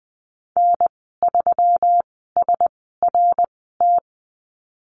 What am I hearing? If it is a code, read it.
Morse code, heard as a keyed tone, about 20 wpm, spaced as for 15 wpm, 700 Hz. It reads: N3SRT
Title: 173 Lesson 28g - Number 9 - Call Signs 20wpm